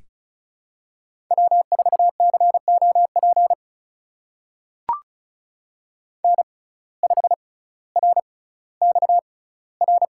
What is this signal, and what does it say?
Morse code, heard as a keyed tone, about 36 wpm, spaced as for 18 wpm, 700 Hz. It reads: W4COP N5RXR
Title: USA Call Signs - 5 Characters - 6x Character Spacing - 35wpm